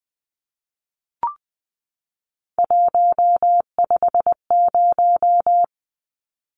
Morse code, heard as a keyed tone, about 19 wpm, 700 Hz. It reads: 150
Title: Numbers - 3-Digit 20wpm